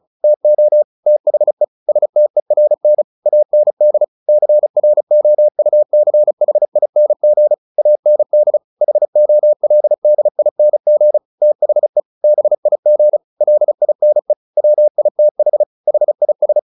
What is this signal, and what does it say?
Morse code, heard as a keyed tone, about 34 wpm, 600 Hz. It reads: TO THE STERN AND CROUCHING AND HOLDING THE BIG LINE WITH HIS
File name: the old man and the sea 2 35wpm